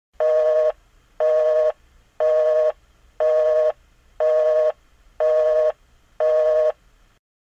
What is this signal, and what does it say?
Busy tone (North America), 480 + 620 Hz.
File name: Line Is Busy